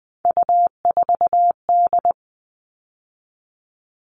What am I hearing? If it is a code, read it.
Morse code, heard as a keyed tone, about 20 wpm, 700 Hz. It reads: U4D
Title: All Cut Numbers - 3-Digit 20wpm